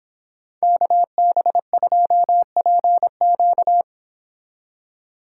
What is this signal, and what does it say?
Morse code, heard as a keyed tone, about 29 wpm, 700 Hz. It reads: KB2PQ